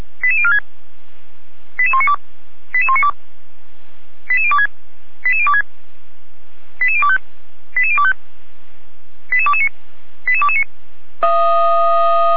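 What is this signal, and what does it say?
Five-tone selective call (ZVEI), heard as quick runs of five tones, tones of about 70 ms. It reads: 88046 88192 88192 88026 88026 88025 88025 88229 88229
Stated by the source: ZVEI five-tone call received by SDR, from feuerwehr-orlamuende.org (Orlamünde fire brigade).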